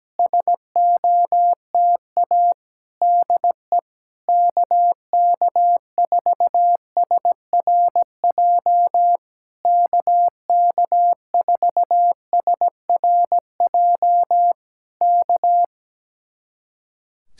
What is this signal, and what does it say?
Morse code, heard as a keyed tone, about 17 wpm, 700 Hz. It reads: SOTA DE KK4SRJ KK4SRJ K
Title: Speed-Racing Common QSO Phrases Encore - 17wpm